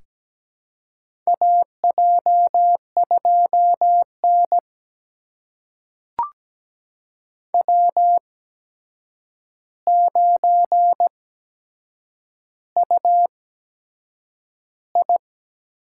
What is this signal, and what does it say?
Morse code, heard as a keyed tone, about 17 wpm, spaced as for 7 wpm, 700 Hz. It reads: AJ2N W9UI
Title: USA Call Signs - 4 Characters - 8x Character Spacing - 17wpm